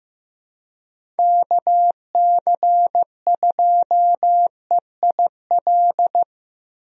Morse code, heard as a keyed tone, about 15 wpm, 700 Hz. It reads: KC2EIL
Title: USA Call Signs - 6 Characters - Repeated 2x - 15wpm